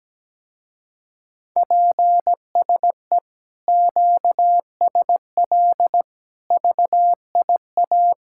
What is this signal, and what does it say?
Morse code, heard as a keyed tone, about 17 wpm, 700 Hz. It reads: PSE QSL VIA
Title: Speed-Racing Common QSO Phrases - 17wpm